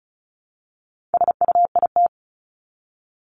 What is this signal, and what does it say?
Morse code, heard as a keyed tone, about 34 wpm, 700 Hz. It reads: SUIT
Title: Top 1501-1750 Words 35wpm